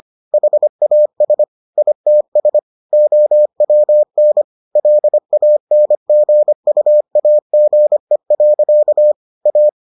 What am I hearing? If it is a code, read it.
Morse code, heard as a keyed tone, about 25 wpm, 600 Hz. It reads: HAS ITS OWN LANGUAGE. A